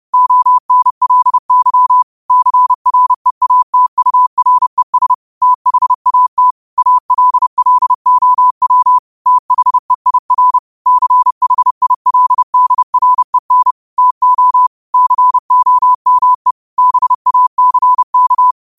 Morse code, heard as a keyed tone, about 30 wpm, 1000 Hz. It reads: ONLY CREATURES THAT ALLOW THEIR CHILDREN TO COME BACK
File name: qotdfiles / 30wpm